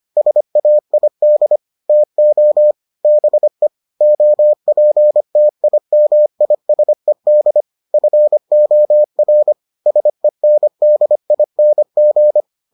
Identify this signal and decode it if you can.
Morse code, heard as a keyed tone, about 25 wpm, 600 Hz. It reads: SAID TO BE OPTIMISED FOR SENDING